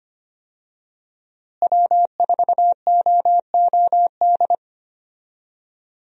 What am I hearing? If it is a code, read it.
Morse code, heard as a keyed tone, about 25 wpm, 700 Hz. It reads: W4OOD
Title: USA Call Signs - 5 Characters - Repeated 2x - 25wpm